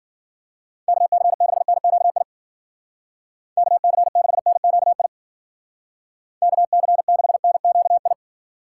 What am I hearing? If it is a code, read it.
Morse code, heard as a keyed tone, about 53 wpm, 700 Hz. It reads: KC6NQI KC6NQI KC6NQI